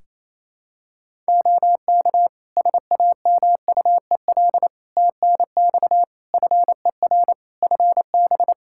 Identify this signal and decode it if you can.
Morse code, heard as a keyed tone, about 28 wpm, 700 Hz. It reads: OK SAMUEL TNX FER FB
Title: K1USN SST QSO Practice 28wpm